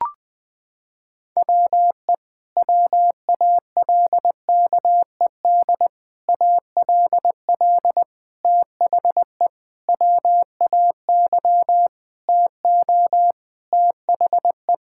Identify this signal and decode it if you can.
Morse code, heard as a keyed tone, about 20 wpm, 700 Hz. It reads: WE WALKED ALL THE WAY TO THE
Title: Sentences with 4-Word Phrases - 20wpm